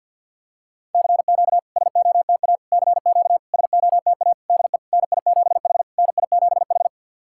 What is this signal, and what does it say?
Morse code, heard as a keyed tone, about 50 wpm, 700 Hz. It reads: CQ SOTA CQ SOTA DE NI7H NI7H
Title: SOTA QSO Practice 50wpm